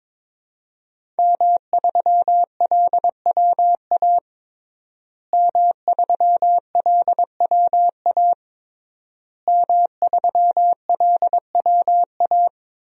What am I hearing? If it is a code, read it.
Morse code, heard as a keyed tone, about 22 wpm, 700 Hz. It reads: M3LWA M3LWA M3LWA
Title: DXCC-100 Call Signs - 5 Characters - Repeated 3x - 22wpm